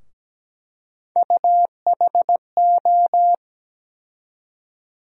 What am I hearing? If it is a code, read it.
Morse code, heard as a keyed tone, about 17 wpm, 700 Hz. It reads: UHO
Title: USA Call Sign Postfixes - 3 Letters 17wpm